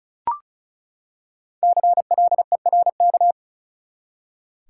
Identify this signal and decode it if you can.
Morse code, heard as a keyed tone, about 36 wpm, 700 Hz. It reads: CLERK
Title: Five Letter Words - 35wpm